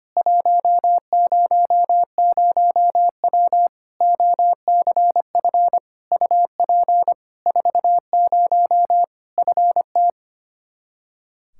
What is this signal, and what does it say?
Morse code, heard as a keyed tone, about 25 wpm, 700 Hz. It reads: 100W OCF UP 40 FT
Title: K1USN SST QSO Practice 25wpm